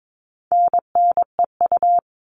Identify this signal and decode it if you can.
Morse code, heard as a keyed tone, about 22 wpm, 700 Hz. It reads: NNEU
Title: All Cut Numbers - 4-Digit 22wpm